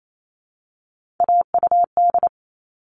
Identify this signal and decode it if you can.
Morse code, heard as a keyed tone, about 28 wpm, 700 Hz. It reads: AUD
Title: All Cut Numbers - 3-Digit 28wpm